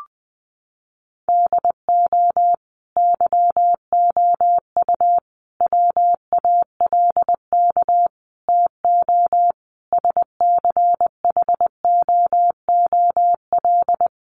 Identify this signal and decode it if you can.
Morse code, heard as a keyed tone, about 20 wpm, 700 Hz. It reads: DO YOU WALK TO SCHOOL
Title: Sentences from Top 1400 Words 20wpm